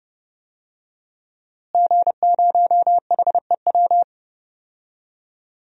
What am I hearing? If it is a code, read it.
Morse code, heard as a keyed tone, about 32 wpm, 700 Hz. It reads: G0HEW
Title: DXCC-100 Call Signs - 5 Characters - Repeated 3x - 30wpm